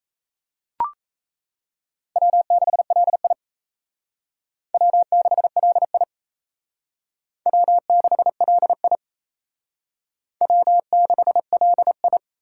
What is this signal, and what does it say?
Morse code, heard as a keyed tone, about 34 wpm, 700 Hz. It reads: W6LI W6LI W6LI W6LI
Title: Speed Racing - USA 1x2 Call Signs 28wpm